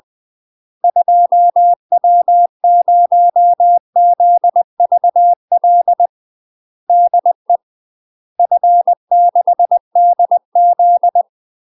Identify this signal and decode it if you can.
Morse code, heard as a keyed tone, about 20 wpm, 700 Hz. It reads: 2W0ZVL DE F6DZ
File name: UK MASTER QSOs TRAINING / 20WPM / UK MASTER QSOs 0063